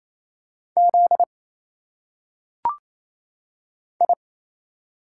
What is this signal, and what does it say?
Morse code, heard as a keyed tone, about 27 wpm, 700 Hz. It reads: Z I
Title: Single Letters - Phonetic Pronunciation 28wpm